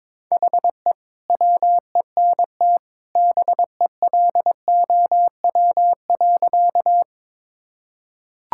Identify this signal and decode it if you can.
Morse code, heard as a keyed tone, about 22 wpm, 700 Hz. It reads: HE WENT BELOW.
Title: Sentences from Top 1000 Words - Review Part 5 - 22wpm